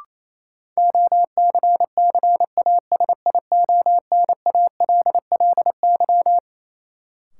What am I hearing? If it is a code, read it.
Morse code, heard as a keyed tone, about 28 wpm, 700 Hz. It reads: OCCASIONALLY
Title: Sentences from Top 1300 Words - New Word Drill - 28wpm